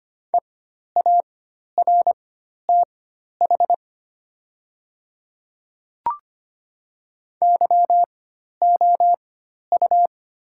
Morse code, heard as a keyed tone, about 25 wpm, spaced as for 12 wpm, 700 Hz. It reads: EARTH YOU
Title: 4x Character Spacing - Top 200 Words - 25wpm